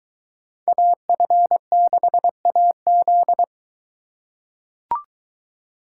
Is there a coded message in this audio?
Yes: Morse code from 0.7 to 3.4 s, heard as a keyed tone, about 21 wpm, 700 Hz. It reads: AF6AZ